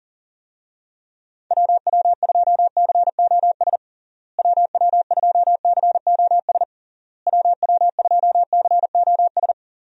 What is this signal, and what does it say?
Morse code, heard as a keyed tone, about 40 wpm, 700 Hz. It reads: WW2COS WW2COS WW2COS
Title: USA Call Signs - 6 Characters - Repeated 3x - 40wpm